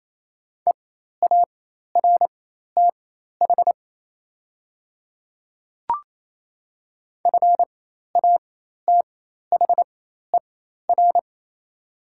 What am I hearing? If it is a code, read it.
Morse code, heard as a keyed tone, about 28 wpm, spaced as for 13 wpm, 700 Hz. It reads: EARTH FATHER
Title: 4x Character Spacing - Top 200 Words - 28wpm